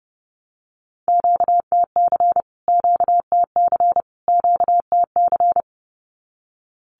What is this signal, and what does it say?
Morse code, heard as a keyed tone, about 30 wpm, 700 Hz. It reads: QTC QTC QTC
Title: Q-Codes Repeated 3x 30wpm